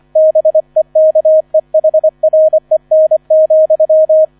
Morse code, heard as a keyed tone, about 25 wpm, 630 Hz. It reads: BEKEHREN,